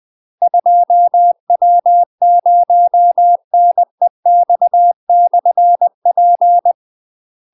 Morse code, heard as a keyed tone, about 20 wpm, 700 Hz. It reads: 2W0NEX/P
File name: UK MASTER QSOs TRAINING / 20WPM / UK MASTER QSOs 0017